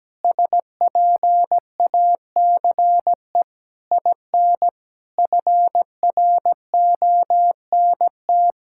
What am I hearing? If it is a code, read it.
Morse code, heard as a keyed tone, about 17 wpm, 700 Hz. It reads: SPACE IN FRONT
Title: Sentences from Top 1000 Words - Review Part 5 - 17wpm